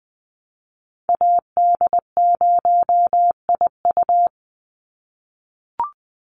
Morse code, heard as a keyed tone, about 20 wpm, 700 Hz. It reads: AD0IU
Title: Call Signs - 20wpm